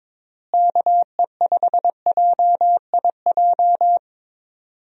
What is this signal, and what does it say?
Morse code, heard as a keyed tone, about 22 wpm, 700 Hz. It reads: KE5JIJ
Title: USA Call Signs - 6 Characters - Repeated 3x - 22wpm